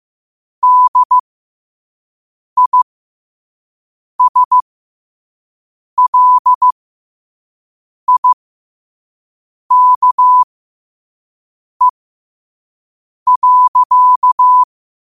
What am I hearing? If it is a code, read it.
Morse code, heard as a keyed tone, about 15 wpm, spaced as for 5 wpm, 1000 Hz. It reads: DISLIKE.
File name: qotdfiles / 5wpm